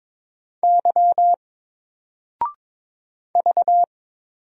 Morse code, heard as a keyed tone, about 22 wpm, 700 Hz. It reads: Y V
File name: Single-Letter-Number-22wpm